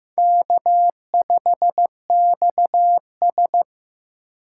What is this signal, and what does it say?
Morse code, heard as a keyed tone, about 15 wpm, 700 Hz. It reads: K5XS
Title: USA Call Signs - 4 Characters - 3x Character Spacing - 15wpm